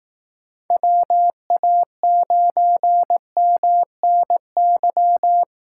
Morse code, heard as a keyed tone, about 18 wpm, 700 Hz. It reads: WA9MNY
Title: Speed Racing - USA 2x3 Call Signs 15wpm